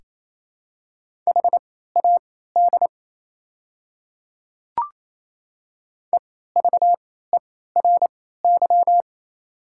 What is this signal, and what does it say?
Morse code, heard as a keyed tone, about 28 wpm, spaced as for 16 wpm, 700 Hz. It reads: HAD EVERY
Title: 3x Character Spacing - Top 200 Words - 28wpm